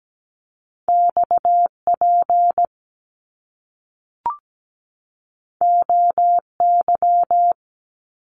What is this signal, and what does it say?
Morse code, heard as a keyed tone, about 17 wpm, 700 Hz. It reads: XP OY
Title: Top Two Letter Combinations in Words 17wpm